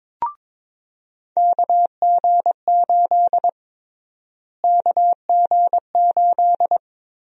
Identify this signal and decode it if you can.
Morse code, heard as a keyed tone, about 22 wpm, 700 Hz. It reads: KG8 KG8
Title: USA Call Sign Prefixes - Repeated 2x - 22wpm